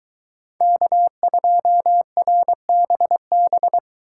Morse code, heard as a keyed tone, about 22 wpm, 700 Hz. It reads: K2RBB